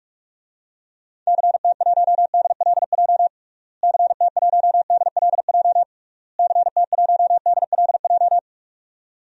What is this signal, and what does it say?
Morse code, heard as a keyed tone, about 43 wpm, 700 Hz. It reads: CT1DLJ CT1DLJ CT1DLJ
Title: DXCC-100 Call Signs - 6 Characters - Repeated 3x - 45wpm